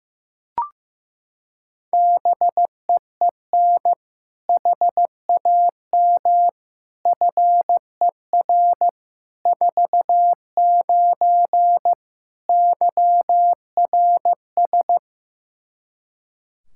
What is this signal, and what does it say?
Morse code, heard as a keyed tone, about 16 wpm, 700 Hz. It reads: BEEN HAM FER 49 YRS
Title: Practice QSO - Exchange 3 - More Info - 15wpm